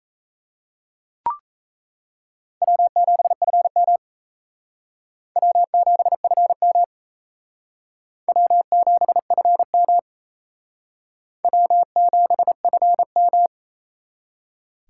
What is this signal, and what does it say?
Morse code, heard as a keyed tone, about 34 wpm, 700 Hz. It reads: W7FM W7FM W7FM W7FM
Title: Speed Racing - USA 1x2 Call Signs 28wpm